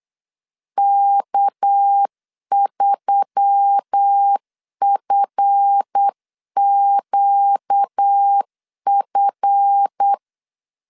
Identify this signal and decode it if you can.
Morse code, heard as a keyed tone, about 8 wpm, 800 Hz. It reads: K3FQF
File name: LEZIONI / 7 GF9Z / 7 GF 8wpm 3108